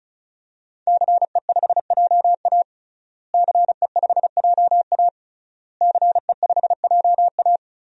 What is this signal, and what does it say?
Morse code, heard as a keyed tone, about 34 wpm, 700 Hz. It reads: CE5JA CE5JA CE5JA
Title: DXCC-100 Call Signs - 5 Characters - Repeated 3x - 35wpm